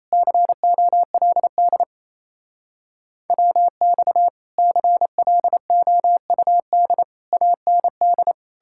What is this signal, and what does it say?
Morse code, heard as a keyed tone, about 36 wpm, 700 Hz. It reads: COLD WX CLOUD AND